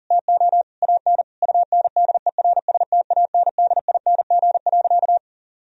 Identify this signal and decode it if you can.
Morse code, heard as a keyed tone, about 40 wpm, 700 Hz. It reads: TO AN UNDERSTANDING.
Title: Sentences from Top 1000 Words - Review Part 3 - 40wpm